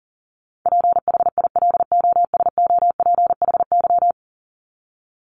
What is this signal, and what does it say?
Morse code, heard as a keyed tone, about 40 wpm, 700 Hz. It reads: PHILOSOPHY
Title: Top 751-1000 Words 40wpm